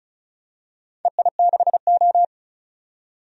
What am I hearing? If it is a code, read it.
Morse code, heard as a keyed tone, about 36 wpm, 700 Hz. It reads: EI6O
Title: DXCC-100 Call Signs - 4 Characters - 35wpm